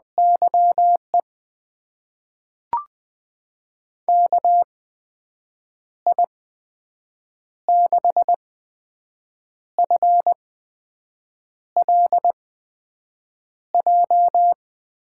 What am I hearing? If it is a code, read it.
Morse code, heard as a keyed tone, about 20 wpm, spaced as for 6 wpm, 700 Hz. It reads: YE KI6FLJ
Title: USA Call Signs - 6 Characters - 8x Character Spacing - 20wpm